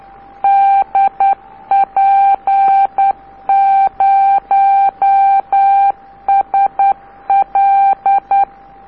Morse code, heard as a keyed tone, about 10 wpm, 790 Hz, with radio noise. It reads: DP0SL
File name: DP0SL cq dp0sl record on tape k by DL4IE